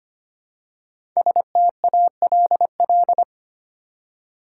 Morse code, heard as a keyed tone, about 25 wpm, 700 Hz. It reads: STALL
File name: 5 Letter Words - First 3 Letters Given - 25wpm